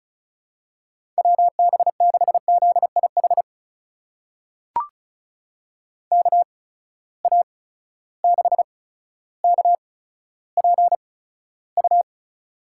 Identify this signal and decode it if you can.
Morse code, heard as a keyed tone, about 36 wpm, spaced as for 15 wpm, 700 Hz. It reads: WB6ZIH KA6KPU
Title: USA Call Signs - 6 Characters - 8x Character Spacing - 35wpm